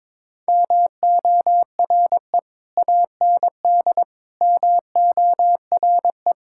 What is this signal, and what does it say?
Morse code, heard as a keyed tone, about 22 wpm, 700 Hz. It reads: MORE AND MORE
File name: practice / 22wpm / Sentences with 3-Word Phrases - 22wpm